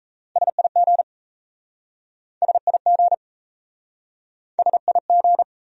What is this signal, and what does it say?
Morse code, heard as a keyed tone, about 38 wpm, 700 Hz. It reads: SIG SIG SIG